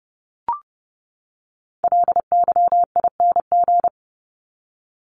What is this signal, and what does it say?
Morse code, heard as a keyed tone, about 30 wpm, 700 Hz. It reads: LYING
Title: ING Suffix - Top 100 Words with Base Words 30wpm